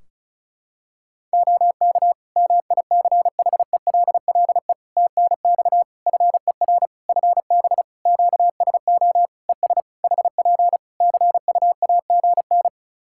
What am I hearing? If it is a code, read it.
Morse code, heard as a keyed tone, about 34 wpm, 700 Hz. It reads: OK MICHELLE TNX FER FB QSO ES HP CUAGN
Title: Practice QSO - Exchange 4 - Ending - 35wpm